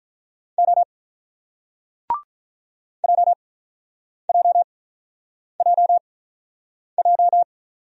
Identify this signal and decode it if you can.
Morse code, heard as a keyed tone, about 48 wpm, spaced as for 9 wpm, 700 Hz. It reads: O JJJJ